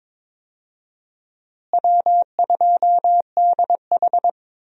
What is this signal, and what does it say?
Morse code, heard as a keyed tone, about 22 wpm, 700 Hz. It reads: W2DH
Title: USA Call Signs - 4 Characters - Repeated 3x - 22wpm